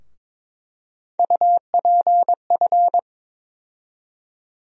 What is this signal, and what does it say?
Morse code, heard as a keyed tone, about 22 wpm, 700 Hz. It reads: UPF
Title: USA Call Sign Postfixes - 3 Letters 22wpm